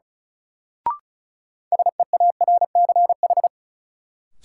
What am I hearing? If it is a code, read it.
Morse code, heard as a keyed tone, about 36 wpm, 700 Hz. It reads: SEARCH
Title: Sentences from Top 800 Words - New Word Drill - 35wpm